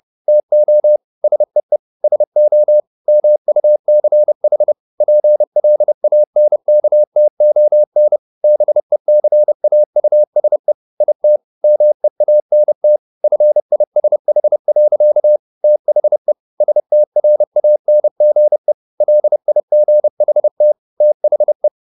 Morse code, heard as a keyed tone, about 30 wpm, 600 Hz. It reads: TO SEE SO MUCH PLANKTON BECAUSE IT MEANT FISH. THE STRANGE LIGHT THE
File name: the old man and the sea / the old man and the sea 1 30wpm